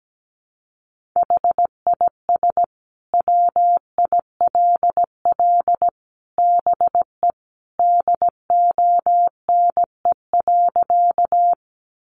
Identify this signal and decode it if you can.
Morse code, heard as a keyed tone, about 17 wpm, 700 Hz. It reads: HIS WILL BE DONE.
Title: Sentences from Top 500 Words 17wpm